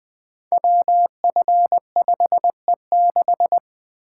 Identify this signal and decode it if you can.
Morse code, heard as a keyed tone, about 20 wpm, 700 Hz. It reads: WF5E6
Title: Code Groups - 5 Letter:Numbers - 20wpm